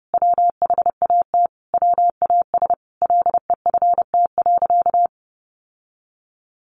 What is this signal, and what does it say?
Morse code, heard as a keyed tone, about 30 wpm, 700 Hz. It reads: WHAT WAS LEFT.
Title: Sentences from Top 1200 Words 30wpm